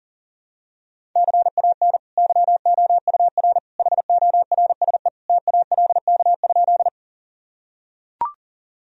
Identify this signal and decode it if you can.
Morse code, heard as a keyed tone, about 40 wpm, 700 Hz. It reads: CAN YOUR HORSE TALK?
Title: Sentences from Top 1000 Words - Review Part 1 - 40wpm